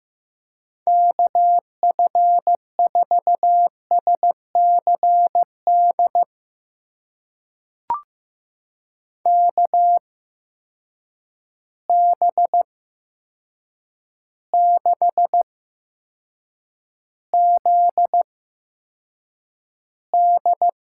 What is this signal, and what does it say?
Morse code, heard as a keyed tone, about 15 wpm, spaced as for 7 wpm, 700 Hz. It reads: KF4SCD KB6ZD
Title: USA Call Signs - 8x Character Spacing - 15wpm